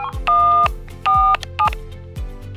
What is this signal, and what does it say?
DTMF, heard as telephone touch tones, tones of about 280 ms. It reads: 147
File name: Iransample / telephone / telephone-demo